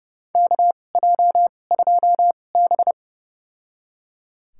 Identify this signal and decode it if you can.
Morse code, heard as a keyed tone, about 30 wpm, spaced as for 22 wpm, 690 Hz. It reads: KJ2B
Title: USA Call Signs - 4 Characters - 2x Character Spacing - 30wpm